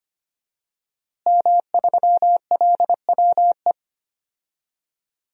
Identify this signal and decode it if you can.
Morse code, heard as a keyed tone, about 25 wpm, 700 Hz. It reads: M3LWE
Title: DXCC-100 Call Signs - 5 Characters - Repeated 2x - 25wpm